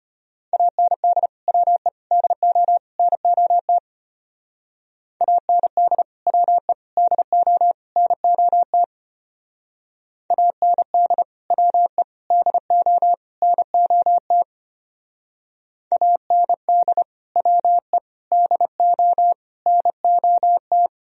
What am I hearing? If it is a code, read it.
Morse code, heard as a keyed tone, about 30 wpm, 700 Hz. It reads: AND WE DO NOT AND WE DO NOT AND WE DO NOT AND WE DO NOT
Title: Speed-Racing - Sets of 4 Words 25wpm